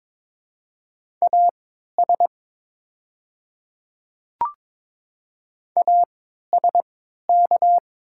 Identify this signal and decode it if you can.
Morse code, heard as a keyed tone, about 22 wpm, spaced as for 12 wpm, 690 Hz. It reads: AS ASK
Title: 3x Character Spacing - Top 300 Words - 22wpm